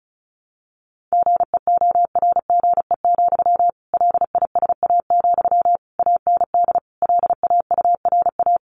Morse code, heard as a keyed tone, about 34 wpm, 700 Hz. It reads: GEORGE, LISA, AND LAURA
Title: Additional Easy Phrases 35wpm